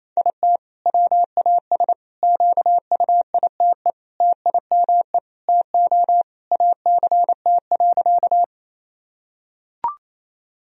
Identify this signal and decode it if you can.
Morse code, heard as a keyed tone, about 28 wpm, 700 Hz. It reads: IT WAS QUITE TIME TO ACT.
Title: Sentences from Top 1000 Words - Review Part 1 - 28wpm